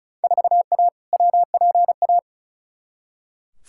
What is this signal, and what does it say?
Morse code, heard as a keyed tone, about 36 wpm, 700 Hz. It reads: 4A WPA
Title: Field Day Exchanges Encore - 35wpm